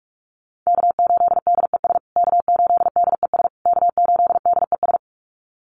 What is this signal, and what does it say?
Morse code, heard as a keyed tone, about 44 wpm, 700 Hz. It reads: K8DES K8DES K8DES
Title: USA Call Signs Repeated 3x 45wpm